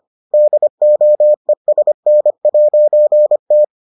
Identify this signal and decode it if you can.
Morse code, heard as a keyed tone, about 25 wpm, 600 Hz. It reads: DOESN'T